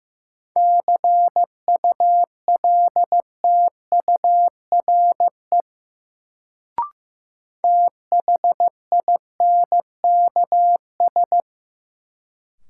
Morse code, heard as a keyed tone, about 15 wpm, 700 Hz. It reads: CULTURE THINKS
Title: Sentences from Top 1000 Words - Review Part 6 - New Word Drill - 15wpm